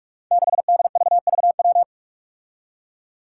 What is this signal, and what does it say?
Morse code, heard as a keyed tone, about 44 wpm, 700 Hz. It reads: 6DVVW
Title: Five Challenging Letter-Number Combinations 45wpm